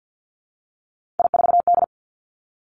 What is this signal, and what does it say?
Morse code, heard as a keyed tone, about 50 wpm, 700 Hz. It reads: I4D
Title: Three Challenging Letter-Number Combinations 50wpm